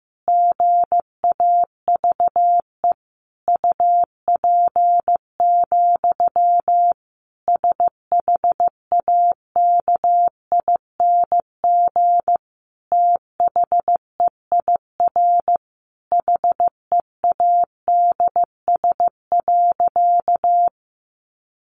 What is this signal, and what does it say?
Morse code, heard as a keyed tone, about 15 wpm, 700 Hz. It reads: GAVE UP, SHAKING THEIR HEADS.
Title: A Sound Of Thunder 15wpm